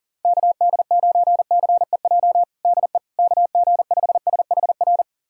Morse code, heard as a keyed tone, about 40 wpm, 700 Hz. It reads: KD9CEJ DE KG5SHR
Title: USA Callsign DE Callsign - 6 Characters - 40wpm